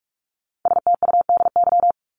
Morse code, heard as a keyed tone, about 44 wpm, 700 Hz. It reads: STUDY
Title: Top 300 Words 45wpm